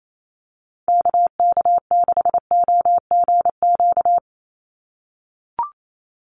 Morse code, heard as a keyed tone, about 28 wpm, 700 Hz. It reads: KK6OGQ
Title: USA Call Signs 28wpm